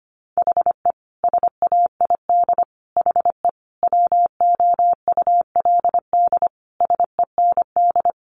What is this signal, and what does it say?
Morse code, heard as a keyed tone, about 25 wpm, 700 Hz. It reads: HE SAID HE WOULD SEND
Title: Sentences from Top 1500 Words 25wpm